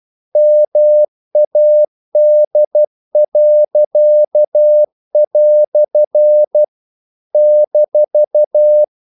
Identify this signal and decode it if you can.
Morse code, heard as a keyed tone, about 12 wpm, 600 Hz. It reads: MAD." -
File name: CW / Practice / EnglishQuotations0112